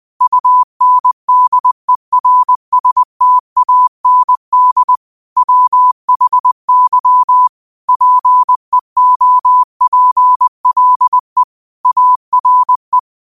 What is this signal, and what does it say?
Morse code, heard as a keyed tone, about 20 wpm, 1000 Hz. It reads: UNDERSTAND WHY PEOPLE ARE